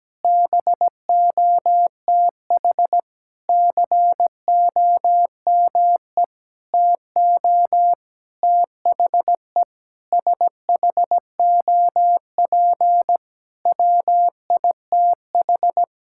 Morse code, heard as a keyed tone, about 17 wpm, 700 Hz. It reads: BOTH COME TO THE SHOP WITH
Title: Sentences from Top 1600 Words 17wpm